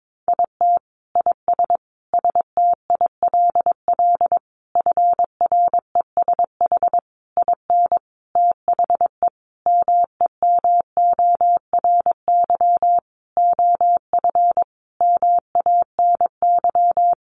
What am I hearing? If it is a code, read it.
Morse code, heard as a keyed tone, about 22 wpm, 700 Hz. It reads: IT IS STILL FRESH IN THE MEMORY OF MANY
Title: Sentences from Top 1300 Words 22wpm